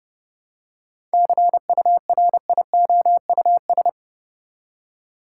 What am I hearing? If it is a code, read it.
Morse code, heard as a keyed tone, about 32 wpm, 700 Hz. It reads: CURIOUS
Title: Seven Letter Words - 30wpm